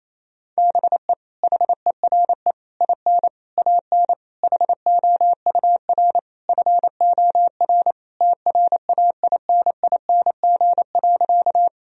Morse code, heard as a keyed tone, about 28 wpm, 700 Hz. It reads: BE HERE IN AN HOUR FOR TRAINING.
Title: Sentences from Top 1000 Words - Review Part 4 - 28wpm